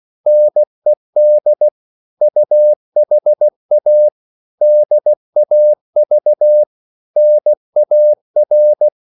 Morse code, heard as a keyed tone, about 16 wpm, 600 Hz. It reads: NED UHA DAV NAR
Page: Korte danske ord 16wpm